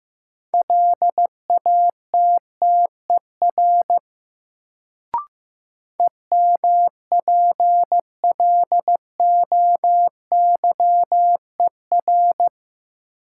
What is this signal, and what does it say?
Morse code, heard as a keyed tone, about 15 wpm, 700 Hz. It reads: LATTER EMPLOYER
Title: Sentences from Top 1600 Words - New Word Drill Alt - 15wpm